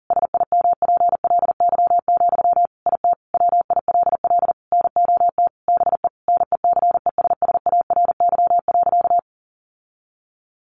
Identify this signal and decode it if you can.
Morse code, heard as a keyed tone, about 40 wpm, 700 Hz. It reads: SIMPLY, IT WILL NOT BE NECESSARY.
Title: Sentences from Top 800 Words 40wpm